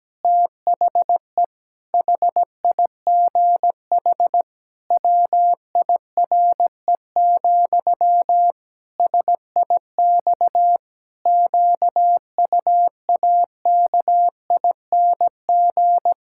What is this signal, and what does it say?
Morse code, heard as a keyed tone, about 17 wpm, 700 Hz. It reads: THE HIGH WIRE, SIX QUAKING
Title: Pangram Challenge v2 - 17wpm